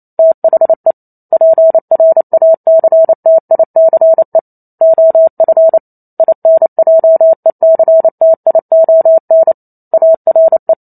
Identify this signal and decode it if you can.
Morse code, heard as a keyed tone, about 28 wpm, 650 Hz. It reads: THE PRACTICE OF INJECTION ARE